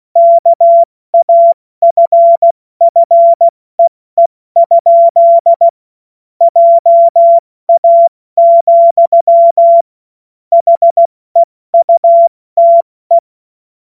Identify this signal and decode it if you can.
Morse code, heard as a keyed tone, about 16 wpm, 680 Hz. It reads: KAFFEE? JA, HEUTE
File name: Morsekurs / Witze / Witz 25. Mai 2021, 70 F80